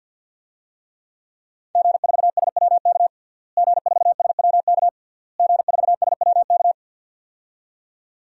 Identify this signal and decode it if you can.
Morse code, heard as a keyed tone, about 50 wpm, 700 Hz. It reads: G4SWK G4SWK G4SWK
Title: DXCC-100 Call Signs - 5 Characters - Repeated 3x - 50wpm